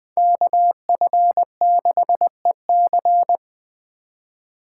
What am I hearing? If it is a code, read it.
Morse code, heard as a keyed tone, about 21 wpm, 700 Hz. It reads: KF6EC